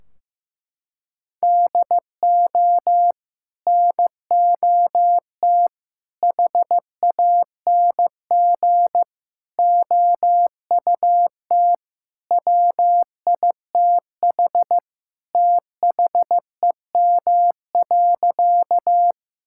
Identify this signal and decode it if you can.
Morse code, heard as a keyed tone, about 16 wpm, 700 Hz. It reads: DO NOT HANG OUT WITH THEM.
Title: Sentences from Top 1600 Words 15wpm